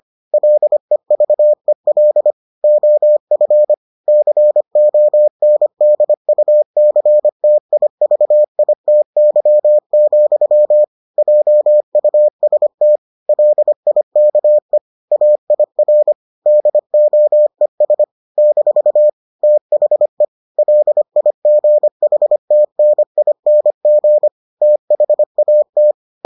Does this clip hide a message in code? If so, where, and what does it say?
Yes: Morse code from 0.3 to 25.9 s, heard as a keyed tone, about 25 wpm, 600 Hz. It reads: LEVEL OF CONDUCTIVITY, JUST LIKE AIR DOES - THE LIGHTNING THAT